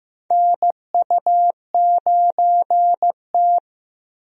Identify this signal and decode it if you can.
Morse code, heard as a keyed tone, about 15 wpm, 700 Hz. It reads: NU9T
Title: USA Call Signs - 4 Characters - Repeated 2x - 15wpm